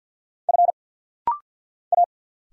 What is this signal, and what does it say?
Morse code, heard as a keyed tone, about 50 wpm, 700 Hz. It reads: F A
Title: Single Letter-Number Alt - Rapid-Fire 50wpm